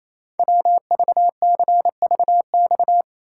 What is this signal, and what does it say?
Morse code, heard as a keyed tone, about 28 wpm, 700 Hz. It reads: WVCVX
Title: Code Groups - 5 Letter:Numbers - 28wpm